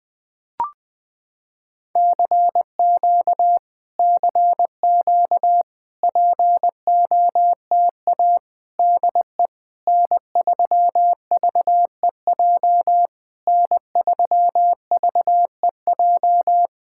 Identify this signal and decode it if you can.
Morse code, heard as a keyed tone, about 20 wpm, 700 Hz. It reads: CQ CQ POTA DE N3VEJ N3VEJ
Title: POTA QSO Practice - 20wpm